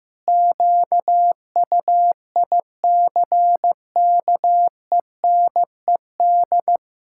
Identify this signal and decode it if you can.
Morse code, heard as a keyed tone, about 15 wpm, 700 Hz. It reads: QUICKENED
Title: Pangram Challenge v2 - 15wpm